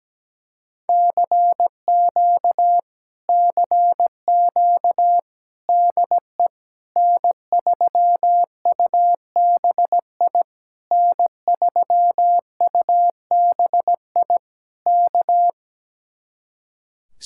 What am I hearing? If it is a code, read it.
Morse code, heard as a keyed tone, about 17 wpm, 700 Hz. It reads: CQ CQ DE N3UBI N3UBI K
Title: Speed-Racing Common QSO Phrases - 17wpm